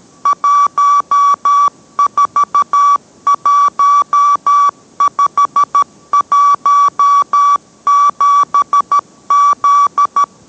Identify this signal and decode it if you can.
Morse code, heard as a keyed tone, about 13 wpm, 1150 Hz. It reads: 141517Z